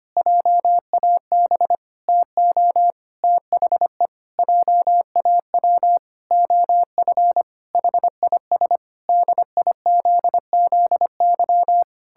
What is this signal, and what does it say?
Morse code, heard as a keyed tone, about 25 wpm, 700 Hz. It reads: JAB TO THE JAW OF HIS DIZZY
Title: Pangram Challenge v2 - 25wpm